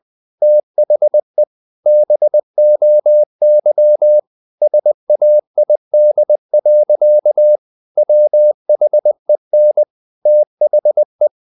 Morse code, heard as a keyed tone, about 20 wpm, 600 Hz. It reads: THE BOY SAID. WHEN THE
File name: the old man and the sea / the old man and the sea part 1 20wpm